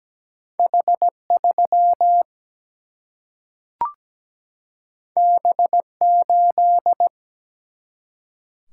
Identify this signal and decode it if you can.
Morse code, heard as a keyed tone, about 17 wpm, 700 Hz. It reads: H 3 B 8
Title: Single-Letter Single-Number 17wpm